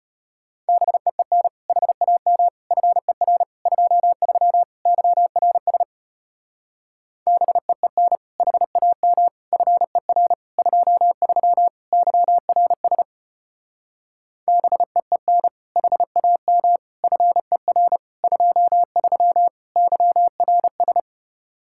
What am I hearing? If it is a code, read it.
Morse code, heard as a keyed tone, about 31 wpm, 700 Hz. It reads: BEEN HAM FER 23 YRS BEEN HAM FER 23 YRS BEEN HAM FER 23 YRS
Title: Speed-Racing Common QSO Phrases - 25wpm